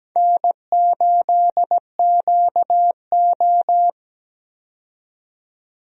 Morse code, heard as a keyed tone, about 17 wpm, 700 Hz. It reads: N8QO
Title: USA Call Signs - 4 Characters - 17wpm